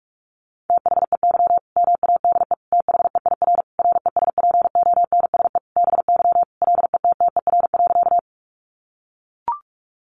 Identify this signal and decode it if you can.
Morse code, heard as a keyed tone, about 44 wpm, 700 Hz. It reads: THEY MADE THEIR RESPONSE BY LETTER.
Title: Sentences from Top 1400 Words 45wpm